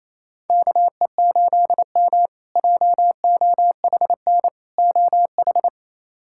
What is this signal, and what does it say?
Morse code, heard as a keyed tone, about 28 wpm, 700 Hz. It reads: KE8M JOHN OH
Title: K1USN SST QSO Practice - 28wpm